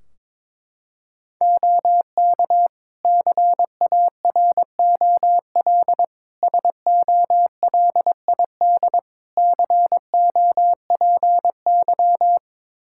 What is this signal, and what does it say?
Morse code, heard as a keyed tone, about 22 wpm, 700 Hz. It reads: OK CAROL SOLID COPY
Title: K1USN SST QSO Practice 22wpm